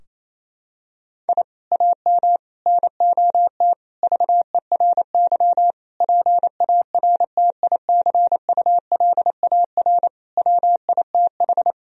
Morse code, heard as a keyed tone, about 28 wpm, 700 Hz. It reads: I AM NOT VERY PARTICULAR WITH
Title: Sentences from Top 1000 Words - Review Part 4 - 28wpm